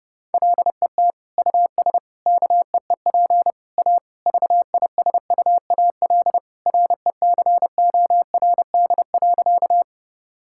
Morse code, heard as a keyed tone, about 30 wpm, 700 Hz. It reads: LET US KEEP A VISUAL RECORD.
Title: Sentences from Top 1000 Words - Review Part 6 - 30wpm